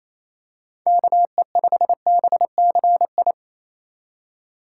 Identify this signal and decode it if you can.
Morse code, heard as a keyed tone, about 28 wpm, 700 Hz. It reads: KE5BCI
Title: USA Call Signs - 6 Characters - 4x Character Spacing - 28wpm